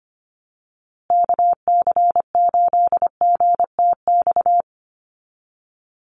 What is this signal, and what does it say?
Morse code, heard as a keyed tone, about 25 wpm, 700 Hz. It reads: KC8GTX
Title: USA Call Signs Repeated 2x 25wpm